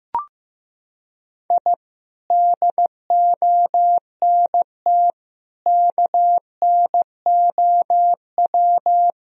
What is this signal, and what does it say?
Morse code, heard as a keyed tone, about 15 wpm, 700 Hz. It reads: I DONT KNOW
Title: Sentences from Top 1000 Words - Review Part 1 - 15wpm